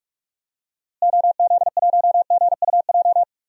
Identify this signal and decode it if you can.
Morse code, heard as a keyed tone, about 46 wpm, 700 Hz. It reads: OZ1GUJ
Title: DXCC-100 Call Signs - 6 Characters - Repeated 3x - 45wpm